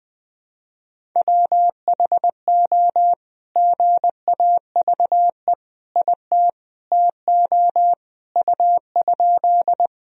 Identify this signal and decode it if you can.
Morse code, heard as a keyed tone, about 21 wpm, 700 Hz. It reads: WHO GAVE IT TO U?
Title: Sentences from Top 500 Words and Abbreviations 20wpm